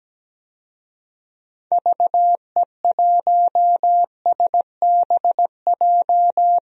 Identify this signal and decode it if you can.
Morse code, heard as a keyed tone, about 17 wpm, 700 Hz. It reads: VE1SBJ
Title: DXCC-100 Call Signs - 6 Characters - 17wpm